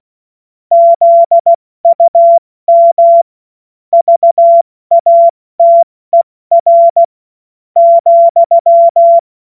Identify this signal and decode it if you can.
Morse code, heard as a keyed tone, about 16 wpm, spaced as for 13 wpm, 680 Hz. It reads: ZUM VATER,
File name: Witz-08-Juli-2025-70-F80